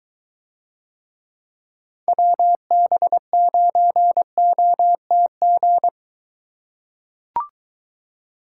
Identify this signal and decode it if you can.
Morse code, heard as a keyed tone, about 22 wpm, 700 Hz. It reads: WB9OTG